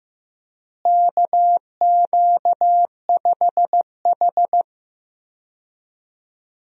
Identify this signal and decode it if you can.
Morse code, heard as a keyed tone, about 15 wpm, 700 Hz. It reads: KQ5H
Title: USA Call Signs - 4 Characters - Repeated 3x - 15wpm